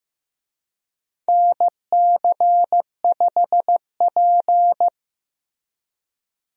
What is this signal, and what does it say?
Morse code, heard as a keyed tone, about 15 wpm, 700 Hz. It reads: NC5P
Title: USA Call Signs - 4 Characters - Repeated 2x - 15wpm